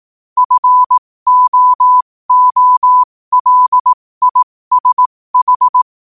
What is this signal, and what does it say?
Morse code, heard as a keyed tone, about 18 wpm, spaced as for 15 wpm, 1000 Hz. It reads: FOOLISH
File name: qotdfiles / 15wpm